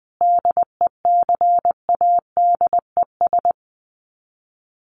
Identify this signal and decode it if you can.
Morse code, heard as a keyed tone, about 20 wpm, 700 Hz. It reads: DECADES
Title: Sets of 5 Words Encore2 20wpm